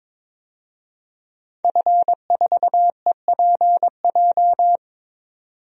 Morse code, heard as a keyed tone, about 22 wpm, 700 Hz. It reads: F4EPJ
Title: DXCC-100 Call Signs - 5 Characters - 22wpm